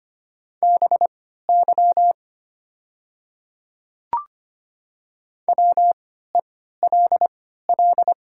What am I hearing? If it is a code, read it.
Morse code, heard as a keyed tone, about 25 wpm, spaced as for 14 wpm, 700 Hz. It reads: BY WELL
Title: 3x Character Spacing - Top 300 Words - 25wpm